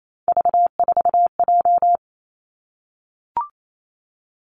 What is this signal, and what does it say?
Morse code, heard as a keyed tone, about 28 wpm, 700 Hz. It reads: V4J
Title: Three Challenging Letter-Number Combinations 28wpm